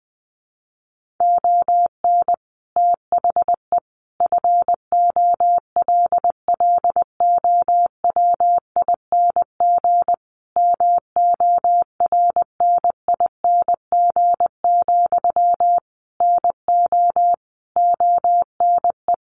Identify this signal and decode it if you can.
Morse code, heard as a keyed tone, about 20 wpm, 700 Hz. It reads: ON THE FOLLOWING MORNING, NO ONE
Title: Sentences from Top 1400 Words 20wpm